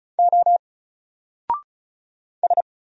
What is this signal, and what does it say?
Morse code, heard as a keyed tone, about 35 wpm, 700 Hz. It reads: O S
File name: Single-Letter-Number-35wpm